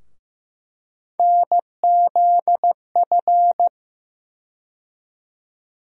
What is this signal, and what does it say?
Morse code, heard as a keyed tone, about 15 wpm, 700 Hz. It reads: NZF
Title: USA Call Sign Postfixes - 3 Letters 15wpm